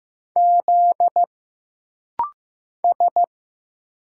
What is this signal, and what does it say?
Morse code, heard as a keyed tone, about 15 wpm, 700 Hz. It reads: Z S
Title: Single Letters - Rapid-Fire 15wpm